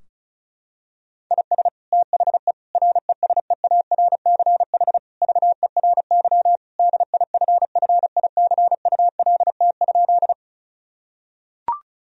Morse code, heard as a keyed tone, about 36 wpm, 700 Hz. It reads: IS THE RESEARCH VERY DIFFICULT?
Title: Sentences from Top 1000 Words - Review Part 4 - 35wpm